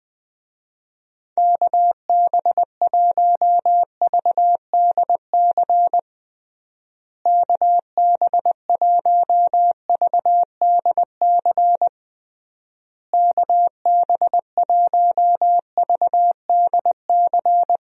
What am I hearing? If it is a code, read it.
Morse code, heard as a keyed tone, about 20 wpm, 700 Hz. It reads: KB1VDC KB1VDC KB1VDC
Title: USA Call Signs - 6 Characters - Repeated 3x - 20wpm